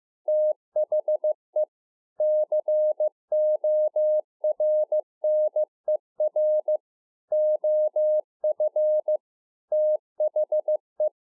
Morse code, heard as a keyed tone, about 15 wpm, 610 Hz. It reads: THE CORNER OF THE